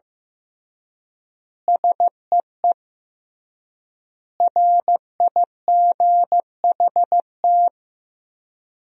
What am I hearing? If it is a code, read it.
Morse code, heard as a keyed tone, about 15 wpm, 700 Hz. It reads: SEE RIGHT
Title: 3x Word-Spacing - Sentences from Top 200 Words - 15wpm